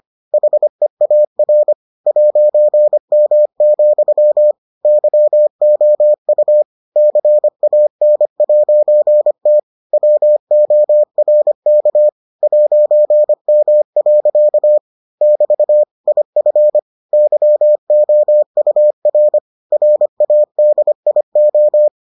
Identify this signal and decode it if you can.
Morse code, heard as a keyed tone, about 25 wpm, 600 Hz. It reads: HEAR 'M, YOU CAN'T WORK 'M. <BT> IF YOUR RADIO